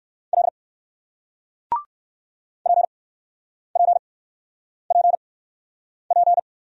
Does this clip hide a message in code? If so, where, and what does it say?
Yes: Morse code from 0.3 to 6.4 s, heard as a keyed tone, about 60 wpm, spaced as for 9 wpm, 700 Hz. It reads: L PPPP